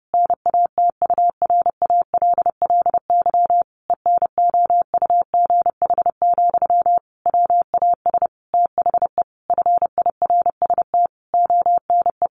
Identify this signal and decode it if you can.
Morse code, heard as a keyed tone, about 28 wpm, 700 Hz. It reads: NATURALLY ENOUGH, WAS THE FIRST ONE
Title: Sentences from Top 1300 Words 30wpm